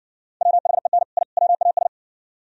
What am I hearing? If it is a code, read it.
Morse code, heard as a keyed tone, about 60 wpm, 700 Hz. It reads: WHEN I WAS